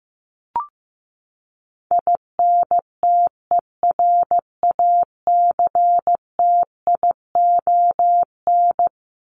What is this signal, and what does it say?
Morse code, heard as a keyed tone, about 15 wpm, 700 Hz. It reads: INTERACTION
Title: Top 1251-1500 Words 15wpm